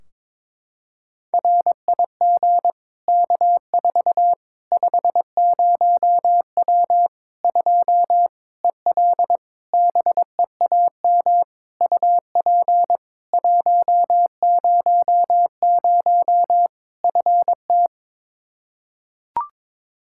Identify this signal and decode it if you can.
Morse code, heard as a keyed tone, about 22 wpm, 700 Hz. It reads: RIG K4 50W 2 EL BEAM UP 100 FT
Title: Practice QSO - Exchange 2 - Info - 22wpm